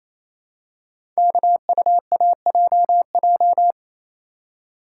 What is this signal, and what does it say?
Morse code, heard as a keyed tone, about 27 wpm, 700 Hz. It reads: KUAJJ
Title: Code Groups - 5 Letters - 28wpm